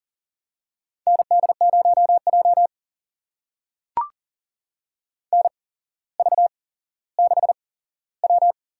Morse code, heard as a keyed tone, about 39 wpm, spaced as for 17 wpm, 700 Hz. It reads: ND0J NV6W
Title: USA Call Signs - 4 Characters - 8x Character Spacing - 40wpm